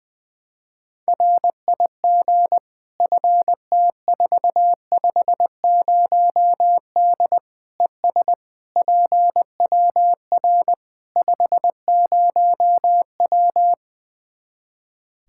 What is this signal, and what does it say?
Morse code, heard as a keyed tone, about 20 wpm, 700 Hz. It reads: RIG FT450D ES PWR 50W
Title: K1USN SST QSO Practice 20wpm